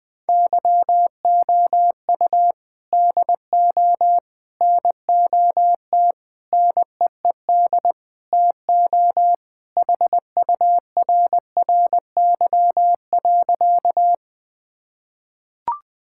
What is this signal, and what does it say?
Morse code, heard as a keyed tone, about 20 wpm, 700 Hz. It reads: YOU DO NOT NEED TO HURRY.
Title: Sentences with 4-Word Phrases - 20wpm